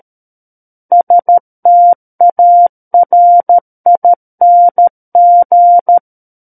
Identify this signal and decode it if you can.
Morse code, heard as a keyed tone, about 13 wpm, 700 Hz. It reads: STARING
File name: Book One / 04-Chapter 4 - 13 WPM